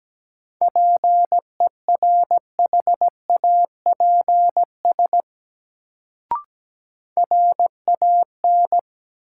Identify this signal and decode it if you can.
Morse code, heard as a keyed tone, about 17 wpm, 700 Hz. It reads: PERHAPS RAN
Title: Sentences from Top 500 Words - New Word Drill - 17wpm